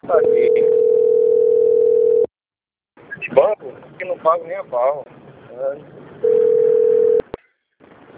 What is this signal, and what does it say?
Ringback tone (North America), 440 + 480 Hz, with speech.
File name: out- - - - - .5918